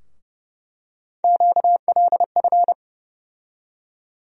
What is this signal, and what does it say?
Morse code, heard as a keyed tone, about 30 wpm, 700 Hz. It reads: QLF
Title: USA Call Sign Postfixes - 3 Letters 30wpm